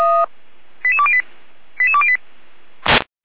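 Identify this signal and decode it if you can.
Five-tone selective call (ZVEI), heard as quick runs of five tones, tones of about 70 ms. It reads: 88208 88208
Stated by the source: ZVEI five-tone selective call received by SDR.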